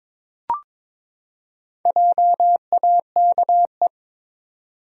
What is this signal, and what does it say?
Morse code, heard as a keyed tone, about 23 wpm, 700 Hz. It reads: JAKE
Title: USA Common Names - Max 4 Letters - 22wpm